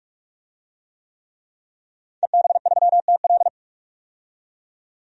Morse code, heard as a keyed tone, about 42 wpm, 700 Hz. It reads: EB3TL
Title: DXCC-100 Call Signs - 5 Characters - 45wpm